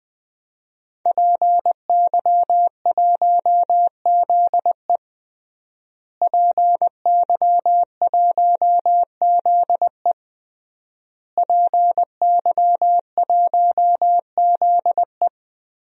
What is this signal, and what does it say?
Morse code, heard as a keyed tone, about 20 wpm, 700 Hz. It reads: PY1ZE PY1ZE PY1ZE
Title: DXCC-100 Call Signs - 5 Characters - Repeated 3x - 20wpm